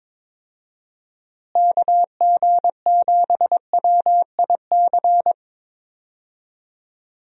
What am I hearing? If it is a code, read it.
Morse code, heard as a keyed tone, about 22 wpm, 700 Hz. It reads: KG7WIC
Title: USA Call Signs - 6 Characters - Repeated 2x - 22wpm